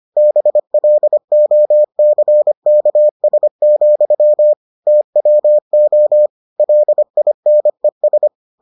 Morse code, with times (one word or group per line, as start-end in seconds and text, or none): 0.2-4.5 BLOCKS,
4.9-6.3 TWO
6.6-8.3 LINES